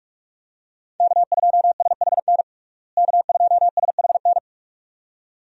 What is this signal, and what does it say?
Morse code, heard as a keyed tone, about 46 wpm, 700 Hz. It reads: K2SHN K2SHN
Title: USA Call Signs - 5 Characters - Repeated 2x - 45wpm